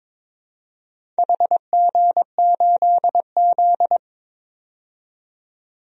Morse code, heard as a keyed tone, about 22 wpm, 700 Hz. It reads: HG8Z
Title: DXCC-100 Call Signs - 4 Characters - 22wpm